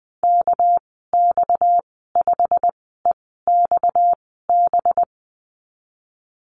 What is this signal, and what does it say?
Morse code, heard as a keyed tone, about 20 wpm, spaced as for 15 wpm, 700 Hz. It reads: KX5EXB
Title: 238 Lesson 38g - Letter q - Call Signs 20wpm